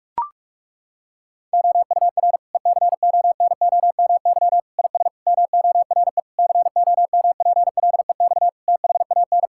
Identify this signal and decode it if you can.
Morse code, heard as a keyed tone, about 41 wpm, 700 Hz. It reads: OUR ECONOMY IS MORE COMPLEX THAN
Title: Sentences from Top 1000 Words - Review Part 6 - 45wpm